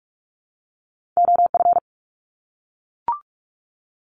Morse code, heard as a keyed tone, about 44 wpm, 700 Hz. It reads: OF
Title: Top 200 Words 45wpm